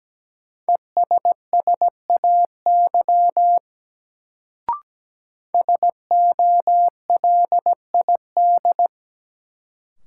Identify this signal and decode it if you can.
Morse code, heard as a keyed tone, about 17 wpm, 700 Hz. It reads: ESSAY SOLID
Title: Sentences from Top 1500 Words - New Word Drill - 17wpm